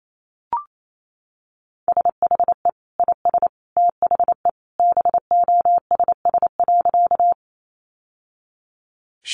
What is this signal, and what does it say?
Morse code, heard as a keyed tone, about 28 wpm, 700 Hz. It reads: SHE IS THE BOSS.
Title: Sentences from Top 700 Words 28wpm